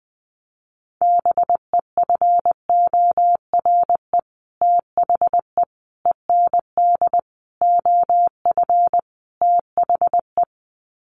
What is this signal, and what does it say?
Morse code, heard as a keyed tone, about 20 wpm, 700 Hz. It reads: BEFORE THE END OF THE
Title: Sets of 5 Words Encore 20wpm